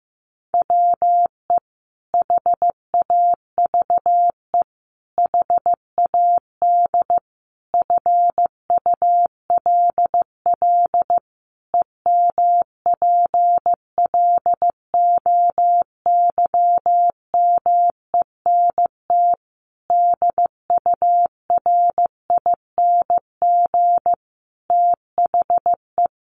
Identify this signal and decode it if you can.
Morse code, heard as a keyed tone, about 15 wpm, 700 Hz. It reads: WE HAVE HAD FULL EMPLOYMENT DURING THE
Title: Sentences from Top 1500 Words 15wpm